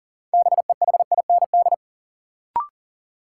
Morse code, heard as a keyed tone, about 40 wpm, 700 Hz. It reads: BEHIND
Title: Sentences from Top 500 Words - New Word Drill - 40wpm